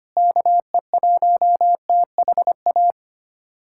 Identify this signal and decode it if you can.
Morse code, heard as a keyed tone, about 25 wpm, 700 Hz. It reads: KE1THA
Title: USA Callsign DE Callsign 25wpm